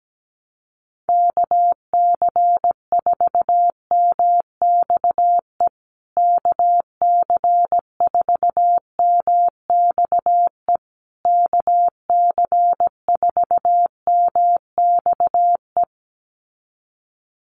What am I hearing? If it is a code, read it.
Morse code, heard as a keyed tone, about 17 wpm, 700 Hz. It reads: KC4MXE KC4MXE KC4MXE
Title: USA Call Signs Repeated 3x 17wpm